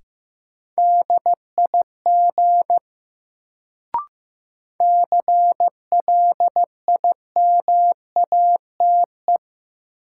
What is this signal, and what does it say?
Morse code, heard as a keyed tone, about 15 wpm, 700 Hz. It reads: DIG CLIMATE
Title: Sentences from Top 1600 Words - New Word Drill - 15wpm